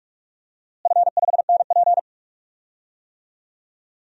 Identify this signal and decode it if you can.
Morse code, heard as a keyed tone, about 46 wpm, 700 Hz. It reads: F5NP
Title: DXCC-100 Call Signs - 4 Characters - Repeated 3x - 45wpm